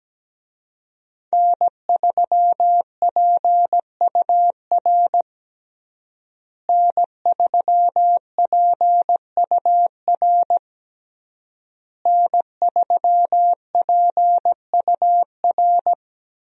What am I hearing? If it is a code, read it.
Morse code, heard as a keyed tone, about 17 wpm, 700 Hz. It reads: N3PUR N3PUR N3PUR
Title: USA Call Signs - 5 Characters - Repeated 3x - 17wpm